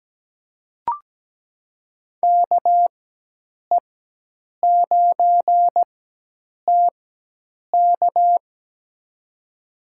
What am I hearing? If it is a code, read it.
Morse code, heard as a keyed tone, about 17 wpm, spaced as for 8 wpm, 700 Hz. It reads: KE9TK
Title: USA Call Signs - 4x Character Spacing - 17wpm